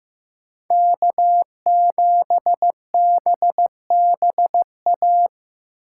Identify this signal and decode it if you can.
Morse code, heard as a keyed tone, about 18 wpm, 700 Hz. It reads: K7BBA
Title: Speed Racing - USA 1x3 Call Signs 15wpm